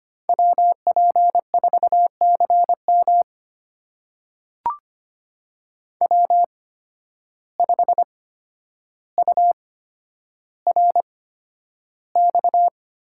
Morse code, heard as a keyed tone, about 25 wpm, spaced as for 11 wpm, 700 Hz. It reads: WP4CM W5URX
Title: USA Call Signs - 8x Character Spacing - 25wpm